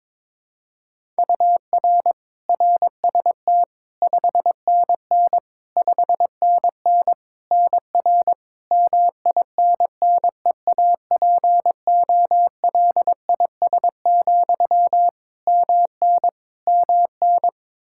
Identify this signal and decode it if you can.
Morse code, heard as a keyed tone, about 22 wpm, 700 Hz. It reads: UR RST 5NN 5NN NR MINNEAPOLIS, MN MN
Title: Practice QSO - Exchange 1 - Beginning - 22wpm